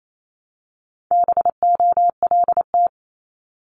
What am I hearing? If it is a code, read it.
Morse code, heard as a keyed tone, about 28 wpm, 700 Hz. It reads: BOLT
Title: Multi-letter Phonemes Repeated 3x 28wpm